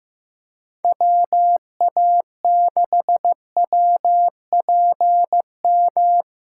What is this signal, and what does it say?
Morse code, heard as a keyed tone, about 15 wpm, 700 Hz. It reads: WA6WPM
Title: USA Callsign DE Callsign - 6 Characters - 15wpm